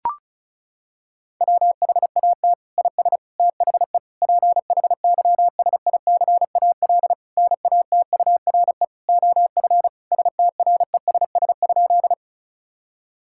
Morse code, heard as a keyed tone, about 36 wpm, 700 Hz. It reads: WHAT IS THE PHYSICAL NATURE OF STRESS?
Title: Sentences from Top 1000 Words - Review Part 6 - 35wpm